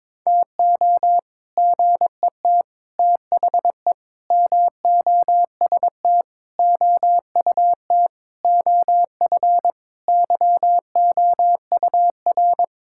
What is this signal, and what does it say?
Morse code, heard as a keyed tone, about 22 wpm, 700 Hz. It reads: TO GET THE MOST OUT OF YOUR
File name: practice / 22wpm / Sets of 7 Words - 22wpm